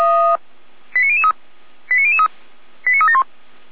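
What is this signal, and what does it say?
Five-tone selective call (ZVEI), heard as quick runs of five tones, tones of about 80 ms. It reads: 89003 89003 89371
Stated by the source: ZVEI five-tone selective call received by SDR.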